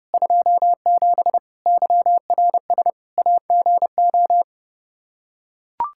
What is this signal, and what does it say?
Morse code, heard as a keyed tone, about 30 wpm, 700 Hz. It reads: 27 YRS AGO
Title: Practice QSO - Exchange 3 - More Info - 30wpm